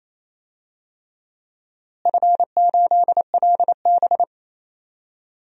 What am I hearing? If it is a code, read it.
Morse code, heard as a keyed tone, about 28 wpm, 700 Hz. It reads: F8LB
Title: DXCC-100 Call Signs - 4 Characters - Repeated 3x - 28wpm